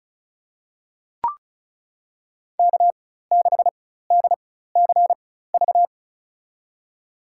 Morse code, heard as a keyed tone, about 34 wpm, spaced as for 16 wpm, 700 Hz. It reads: K6DCV
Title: USA Call Signs - 5 Characters - 4x Character Spacing - 35wpm